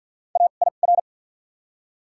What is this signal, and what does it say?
Morse code, heard as a keyed tone, about 53 wpm, spaced as for 37 wpm, 700 Hz. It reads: AIR
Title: 2x Character Spacing - Top 300 Words - 50wpm